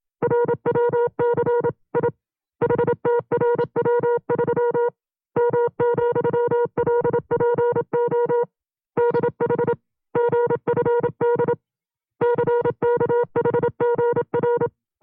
Morse code, heard as a keyed tone, about 27 wpm, 480 Hz. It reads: RWCI HTRW3 M,LPO DH GFD CKHGR
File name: FreeFiles / CW-practice / 25WPM-side2